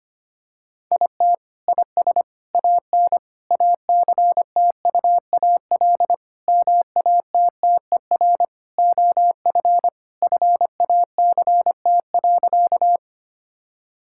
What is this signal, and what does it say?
Morse code, heard as a keyed tone, about 25 wpm, 700 Hz. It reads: IT IS AN ACTUAL MATTER OF FACT.
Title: Sentences from Top 1000 Words - Review Part 5 - 25wpm